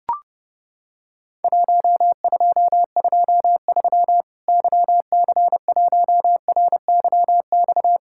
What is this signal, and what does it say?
Morse code, heard as a keyed tone, about 30 wpm, 700 Hz. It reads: 1223 YC1RYX
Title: Worked All Europe DX Contest - QTC Exchange - 30wpm